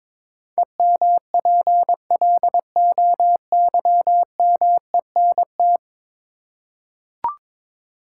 Morse code, heard as a keyed tone, about 21 wpm, 700 Hz. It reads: EMPLOYMENT
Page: morse-code-ninja-practice-22wpm